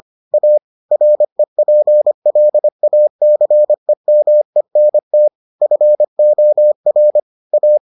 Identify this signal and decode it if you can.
Morse code, heard as a keyed tone, about 25 wpm, 600 Hz. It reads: A REPLACEMENT FOR A